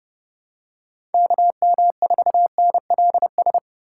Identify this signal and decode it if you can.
Morse code, heard as a keyed tone, about 32 wpm, 700 Hz. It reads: KM4NLS
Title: USA Call Signs - 6 Characters - 6x Character Spacing - 30wpm